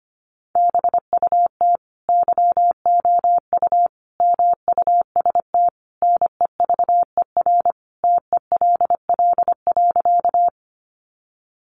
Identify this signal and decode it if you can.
Morse code, heard as a keyed tone, about 25 wpm, 700 Hz. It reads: BUT YOU MUST NEVER TELL.
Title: Sentences from Top 300 Words 25wpm